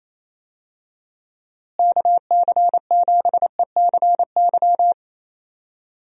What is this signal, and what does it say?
Morse code, heard as a keyed tone, about 28 wpm, 700 Hz. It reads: KC7ECY
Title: USA Call Signs - 6 Characters - Repeated 3x - 28wpm